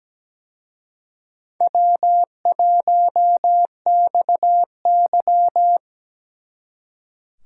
Morse code, heard as a keyed tone, about 20 wpm, 700 Hz. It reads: W1XY
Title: Speed Racing - USA 1x2 Call Signs 17wpm